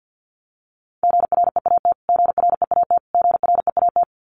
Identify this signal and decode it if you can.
Morse code, heard as a keyed tone, about 50 wpm, 700 Hz. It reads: GREAT GREAT GREAT
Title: Top 500 Words Repeated 50wpm